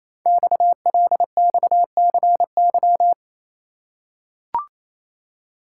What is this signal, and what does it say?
Morse code, heard as a keyed tone, about 27 wpm, 690 Hz. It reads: XLXCY
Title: Code Groups - 5 Letters - 28wpm